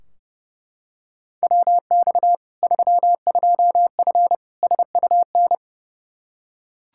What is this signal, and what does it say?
Morse code, heard as a keyed tone, about 30 wpm, 700 Hz. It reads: WX 32F SUN
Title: Practice QSO - Exchange 2 - Info - 30wpm